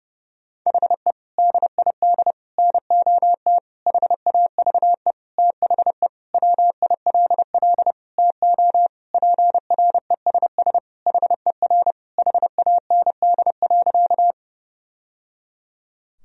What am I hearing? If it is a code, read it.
Morse code, heard as a keyed tone, about 30 wpm, 700 Hz. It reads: HE DID NOT HAVE THE WILL TO PRESS HER HAND.
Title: Sentences from Top 1000 Words - Review Part 1 - 30wpm